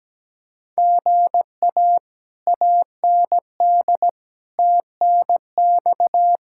Morse code, heard as a keyed tone, about 17 wpm, 700 Hz. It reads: GA AND TNX
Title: Speed-Racing Common QSO Phrases Encore - 17wpm